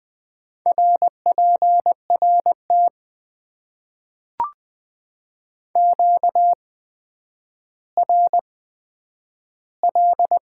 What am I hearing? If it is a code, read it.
Morse code, heard as a keyed tone, about 20 wpm, spaced as for 10 wpm, 700 Hz. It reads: RPRT QRL
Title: Top 90 QSO Elements - 8x Character Spacing - 20wpm